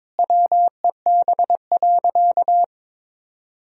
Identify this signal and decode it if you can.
Morse code, heard as a keyed tone, about 22 wpm, 700 Hz. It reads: WEB.
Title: Sentences from Top 1000 Words - Review Part 4 - 22wpm